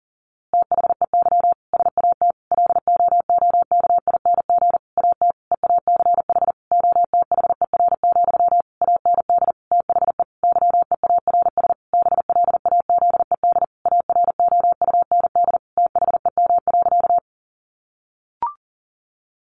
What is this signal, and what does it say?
Morse code, heard as a keyed tone, about 40 wpm, 700 Hz. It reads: THEY SAT LOOKING AT EACH OTHER, AND THE YEARS BLAZED AROUND THEM.
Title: A Sound Of Thunder 40wpm